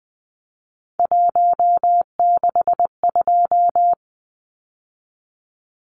Morse code, heard as a keyed tone, about 20 wpm, 700 Hz. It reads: 162
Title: Numbers - 3-Digit 20wpm